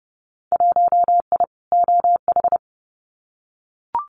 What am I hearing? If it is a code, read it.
Morse code, heard as a keyed tone, about 30 wpm, 700 Hz. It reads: 1I OH
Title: Winter Field Day Exchanges 30wpm